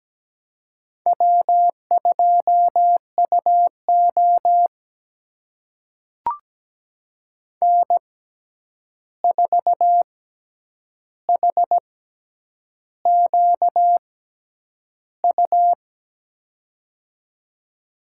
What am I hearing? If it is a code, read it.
Morse code, heard as a keyed tone, about 17 wpm, spaced as for 8 wpm, 700 Hz. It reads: W2UO N4HQU
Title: USA Call Signs - 6x Character Spacing - 17wpm